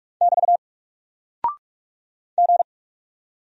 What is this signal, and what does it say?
Morse code, heard as a keyed tone, about 44 wpm, 700 Hz. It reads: <BT> G
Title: Single Character (Letter:Number:Punctuation:Prosign) 45wpm